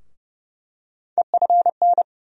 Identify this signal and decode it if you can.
Morse code, heard as a keyed tone, about 30 wpm, 700 Hz. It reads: EFN
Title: USA Call Sign Postfixes - 3 Letters 30wpm